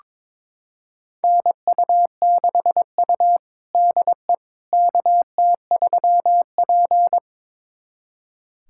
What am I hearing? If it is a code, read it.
Morse code, heard as a keyed tone, about 22 wpm, 700 Hz. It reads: NU6U DE KT3P
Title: USA Callsign DE Callsign - 4 Characters - 22wpm